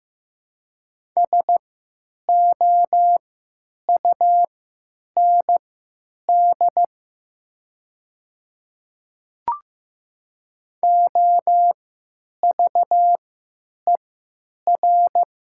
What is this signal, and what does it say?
Morse code, heard as a keyed tone, about 14 wpm, spaced as for 8 wpm, 700 Hz. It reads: SOUND OVER
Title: 3x Character Spacing - Top 100 Words - 15wpm